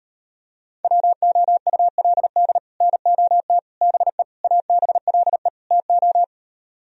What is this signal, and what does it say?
Morse code, heard as a keyed tone, about 39 wpm, 700 Hz. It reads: WOULD NOT BE ABLE TO